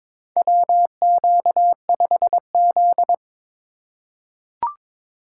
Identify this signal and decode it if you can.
Morse code, heard as a keyed tone, about 22 wpm, 700 Hz. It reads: WQ5Z
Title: USA Call Signs - 3x Character Spacing - 22wpm